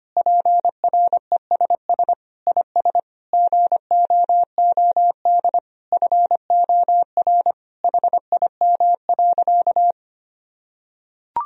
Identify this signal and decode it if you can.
Morse code, heard as a keyed tone, about 25 wpm, 700 Hz. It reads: PRESS IS GOOD FOR HIM.
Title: Sentences from Top 1000 Words - Review Part 1 - 25wpm